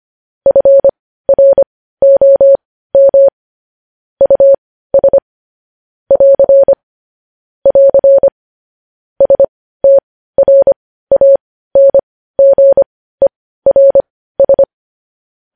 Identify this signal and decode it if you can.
Morse code, heard as a keyed tone, about 25 wpm, spaced as for 15 wpm, 550 Hz. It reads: FROM US <AR> <AR> STRANGERS
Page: news-headlines-in-morse-code-at-15-wpm